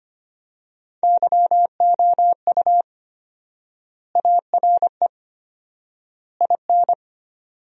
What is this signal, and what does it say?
Morse code, heard as a keyed tone, about 25 wpm, 700 Hz. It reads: YOU ARE IN
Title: 4x Word-Spacing - Sentences from Top 500 Words - 25wpm